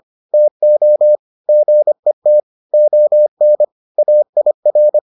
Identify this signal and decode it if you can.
Morse code, heard as a keyed tone, about 25 wpm, 600 Hz. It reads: TO GET ON AIR